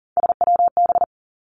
Morse code, heard as a keyed tone, about 37 wpm, 700 Hz. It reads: SWB
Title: Three Challenging Letter-Number Combinations 40wpm